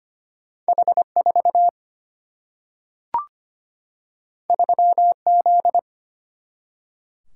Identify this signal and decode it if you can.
Morse code, heard as a keyed tone, about 25 wpm, 700 Hz. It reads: H 4 3 Z
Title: Single-Letter Single-Number 25wpm